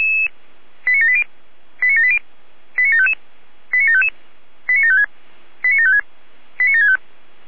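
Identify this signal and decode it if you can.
Five-tone selective call (ZVEI), heard as quick runs of five tones, tones of about 70 ms. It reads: 89780 89780 89755 89755 89756 89756 89765
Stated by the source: ZVEI five-tone selective call received by SDR.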